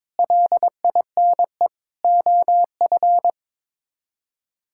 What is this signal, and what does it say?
Morse code, heard as a keyed tone, about 22 wpm, 700 Hz. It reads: LINE OF
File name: practice / 22wpm / Sets of 7 Words - 22wpm